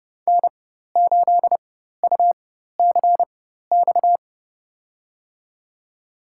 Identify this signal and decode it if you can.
Morse code, heard as a keyed tone, about 30 wpm, spaced as for 14 wpm, 700 Hz. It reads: N8UCX
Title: USA Call Signs - 4x Character Spacing - 30wpm